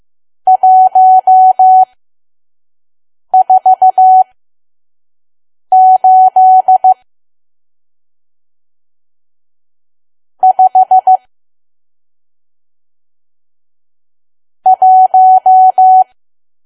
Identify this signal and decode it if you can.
Morse code, heard as a keyed tone, about 15 wpm, spaced as for 5 wpm, 750 Hz. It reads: 148 5 1
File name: Archive / 5 WPM files / PM